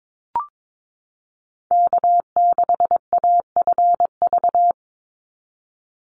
Morse code, heard as a keyed tone, about 22 wpm, 700 Hz. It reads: K6AFV
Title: USA Call Signs 22wpm